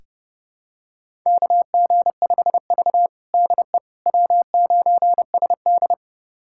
Morse code, heard as a keyed tone, about 32 wpm, 700 Hz. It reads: KG5V DE W9SD
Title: USA Callsign DE Callsign - 4 Characters - 30wpm